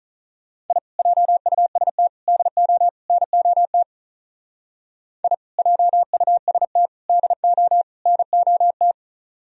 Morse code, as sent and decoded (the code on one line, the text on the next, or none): .. / .--- ..- ... - / -.. --- / -. --- - / .. / .--- ..- ... - / -.. --- / -. --- -
I JUST DO NOT I JUST DO NOT